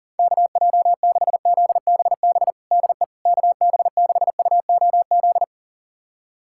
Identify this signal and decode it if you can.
Morse code, heard as a keyed tone, about 40 wpm, 700 Hz. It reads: KJ6ZBB DE KB6UOZ
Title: USA Callsign DE Callsign 40wpm